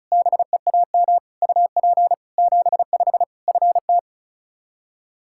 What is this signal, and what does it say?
Morse code, heard as a keyed tone, about 36 wpm, 700 Hz. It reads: BEAM UP 75 FT
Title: Practice QSO - Exchange 2 - Info - 35wpm